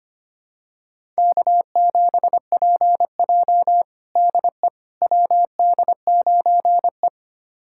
Morse code, heard as a keyed tone, about 25 wpm, 700 Hz. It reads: K7PJ DE WD9E
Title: USA Callsign DE Callsign - 4 Characters - 25wpm